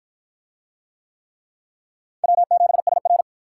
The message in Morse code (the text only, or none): W7SR